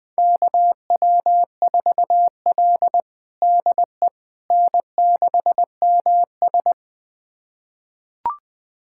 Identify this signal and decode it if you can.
Morse code, heard as a keyed tone, about 20 wpm, 700 Hz. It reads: KW4L DE N6MS
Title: USA Callsign DE Callsign - 4 Characters - 20wpm